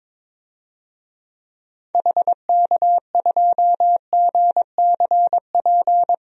Morse code, heard as a keyed tone, about 22 wpm, 700 Hz. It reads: HK2GCP
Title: DXCC-100 Call Signs - 6 Characters - 22wpm